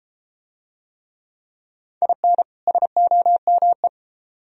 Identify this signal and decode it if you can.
Morse code, heard as a keyed tone, about 33 wpm, 700 Hz. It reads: IN SOME